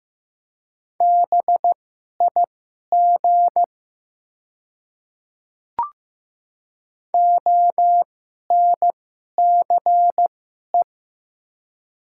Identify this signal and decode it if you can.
Morse code, heard as a keyed tone, about 15 wpm, spaced as for 11 wpm, 700 Hz. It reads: BIG ONCE
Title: 2x Character Spacing - Top 300 Words - 15wpm